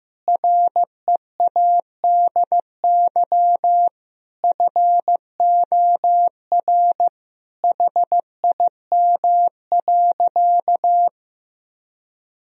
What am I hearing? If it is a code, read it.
Morse code, heard as a keyed tone, about 16 wpm, 700 Hz. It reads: READY FOR HIM.
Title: Sentences from Top 1000 Words - Review Part 3 - 15wpm